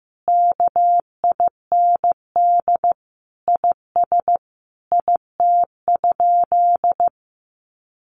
Morse code, heard as a keyed tone, about 14 wpm, 700 Hz. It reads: KIND IS IT?
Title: Sentences from Top 200 Words 15wpm